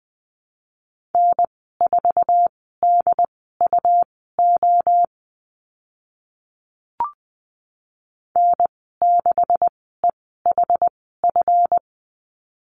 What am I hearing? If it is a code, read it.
Morse code, heard as a keyed tone, about 20 wpm, spaced as for 15 wpm, 700 Hz. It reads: N4DUO N6EHF
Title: 147 Lesson 24g - Letter y - Call Signs 20wpm